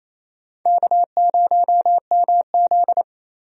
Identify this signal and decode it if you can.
Morse code, heard as a keyed tone, about 28 wpm, 700 Hz. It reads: K0MZ
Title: USA Call Signs - 4 Characters - Repeated 3x - 28wpm